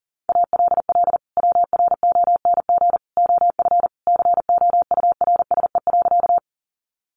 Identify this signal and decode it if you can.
Morse code, heard as a keyed tone, about 40 wpm, 700 Hz. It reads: ALL WRONG OF COURSE.
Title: Sentences from Top 900 Words 40wpm